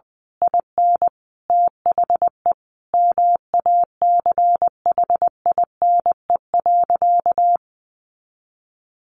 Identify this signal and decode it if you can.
Morse code, heard as a keyed tone, about 20 wpm, 700 Hz. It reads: IN THE MACHINE.
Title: A Sound Of Thunder 20wpm